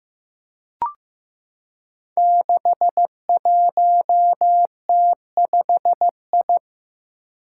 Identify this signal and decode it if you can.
Morse code, heard as a keyed tone, about 15 wpm, 700 Hz. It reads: 61T5I
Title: Five Challenging Letter-Number Combinations 15wpm